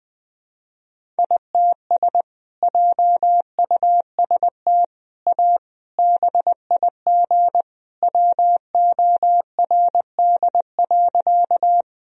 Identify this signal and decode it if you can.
Morse code, heard as a keyed tone, about 20 wpm, 700 Hz. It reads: ITS JUST A BIG WORD.
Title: Sentences from Top 1000 Words - Review Part 1 - 20wpm